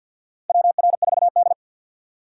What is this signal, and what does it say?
Morse code, heard as a keyed tone, about 54 wpm, 700 Hz. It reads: WR4D
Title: USA Call Signs - 6x Character Spacing - 50wpm